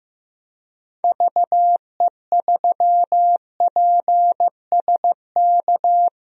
Morse code, heard as a keyed tone, about 15 wpm, 700 Hz. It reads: VE3PSK
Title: DXCC-100 Call Signs - 6 Characters - Repeated 3x - 15wpm